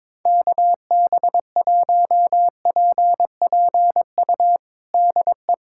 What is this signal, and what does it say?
Morse code, heard as a keyed tone, about 22 wpm, 700 Hz. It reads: KB1PPU DE
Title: USA Callsign DE Callsign - 6 Characters - 22wpm